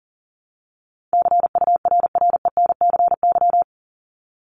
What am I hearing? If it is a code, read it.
Morse code, heard as a keyed tone, about 40 wpm, 700 Hz. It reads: CURRENCY
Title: Top 1251-1500 Words 40wpm